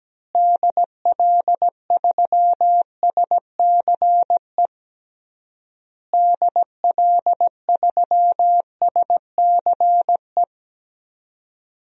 Morse code, heard as a keyed tone, about 17 wpm, 700 Hz. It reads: DL3SCE DL3SCE
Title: DXCC-100 Call Signs - 6 Characters - Repeated 3x - 17wpm